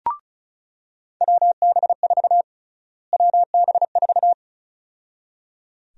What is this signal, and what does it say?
Morse code, heard as a keyed tone, about 34 wpm, 700 Hz. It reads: WB4 WB4
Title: USA Call Sign Prefixes - Repeated 2x - 35wpm